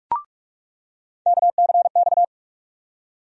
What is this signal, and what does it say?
Morse code, heard as a keyed tone, about 42 wpm, 700 Hz. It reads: KCX
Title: USA Call Sign Postfixes - 3 Letters 45wpm